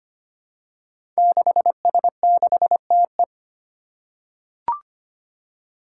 Morse code, heard as a keyed tone, about 25 wpm, 700 Hz. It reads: 6S6TE
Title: Five Challenging Letter-Number Combinations 25wpm